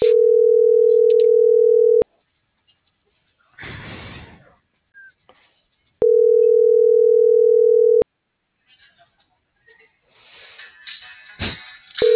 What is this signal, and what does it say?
Ringback tone (North America), 440 + 480 Hz.